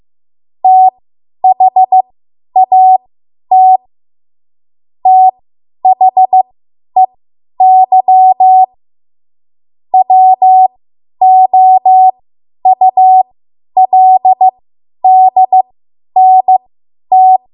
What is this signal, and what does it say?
Morse code, heard as a keyed tone, about 15 wpm, spaced as for 10 wpm, 750 Hz. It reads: THAT THEY WOULDNT
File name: Archive / 10 WPM / 10WPM